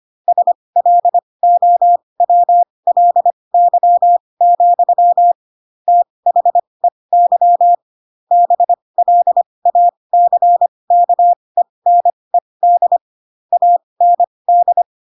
Morse code, heard as a keyed tone, about 25 wpm, spaced as for 20 wpm, 700 Hz. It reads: SLOWLY, THEY BLACKENED AND
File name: Pinocchio / Pinocchio ch6 15 25 ik1hge